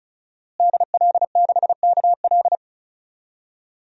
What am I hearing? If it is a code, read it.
Morse code, heard as a keyed tone, about 36 wpm, 700 Hz. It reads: DL6KL
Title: DXCC-100 Call Signs - 5 Characters - Repeated 3x - 35wpm